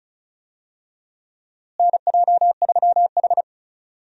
Morse code, heard as a keyed tone, about 34 wpm, 700 Hz. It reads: NJ3H
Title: USA Call Signs - 4 Characters - Repeated 3x - 35wpm